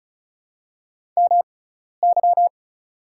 Morse code, heard as a keyed tone, about 36 wpm, spaced as for 12 wpm, 700 Hz. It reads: MY
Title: 6x Character Spacing - Top 100 Words - 35wpm